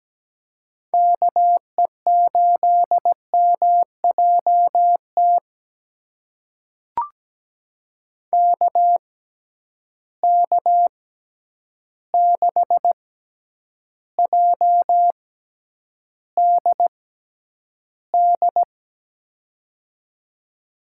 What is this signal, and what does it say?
Morse code, heard as a keyed tone, about 17 wpm, spaced as for 9 wpm, 700 Hz. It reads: KE8MJT KK6JDD
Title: USA Call Signs - 6 Characters - 6x Character Spacing - 17wpm